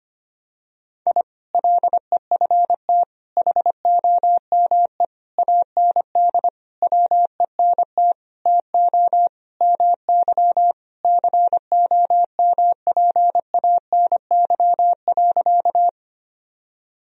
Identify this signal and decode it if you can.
Morse code, heard as a keyed tone, about 25 wpm, 700 Hz. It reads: I LEFT HOME AND WENT TO MY COMPANY.
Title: Sentences from Top 1000 Words - Review Part 3 - 25wpm